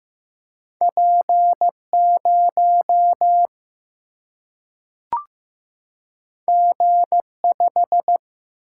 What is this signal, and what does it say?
Morse code, heard as a keyed tone, about 15 wpm, 700 Hz. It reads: P0 G5
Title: Single-Letter Single-Number 15wpm